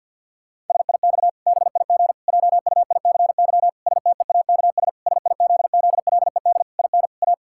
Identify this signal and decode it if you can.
Morse code, heard as a keyed tone, about 51 wpm, 700 Hz. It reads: SIX BIG JUICY STEAKS SIZZLED IN A
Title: Pangram Challenge v2 - 50wpm